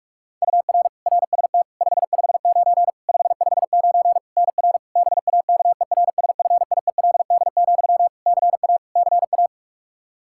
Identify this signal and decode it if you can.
Morse code, heard as a keyed tone, about 42 wpm, 700 Hz. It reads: UR RST 559 559 NR BAKERSFIELD, CA CA
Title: Practice QSO - Exchange 1 - Beginning - 45wpm